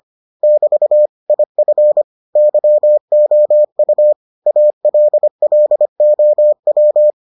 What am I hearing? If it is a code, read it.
Morse code, heard as a keyed tone, about 25 wpm, 600 Hz. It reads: <BT> IF YOU ALLOW